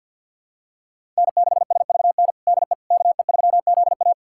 Morse code, heard as a keyed tone, about 52 wpm, 700 Hz. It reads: N6SVN DE KE3ZA
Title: USA Callsign DE Callsign - 5 Characters - 50wpm